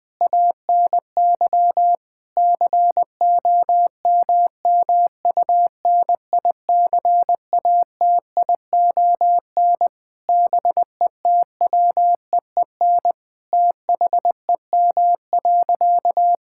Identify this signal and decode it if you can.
Morse code, heard as a keyed tone, about 20 wpm, 700 Hz. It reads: ANY COMMUNICATION BETWEEN THEM.
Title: Sentences from Top 1000 Words - Review Part 6 - 20wpm